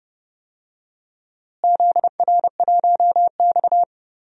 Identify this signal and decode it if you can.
Morse code, heard as a keyed tone, about 32 wpm, 700 Hz. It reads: ZR1X
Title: DXCC-100 Call Signs - 4 Characters - Repeated 3x - 30wpm